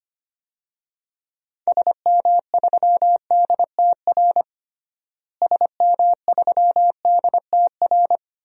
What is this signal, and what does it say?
Morse code, heard as a keyed tone, about 25 wpm, 700 Hz. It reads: SM3DTR SM3DTR
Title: DXCC-100 Call Signs - 6 Characters - Repeated 3x - 25wpm